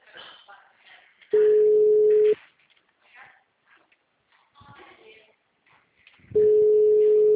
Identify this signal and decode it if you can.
Ringback tone (Europe, 425 Hz), with speech.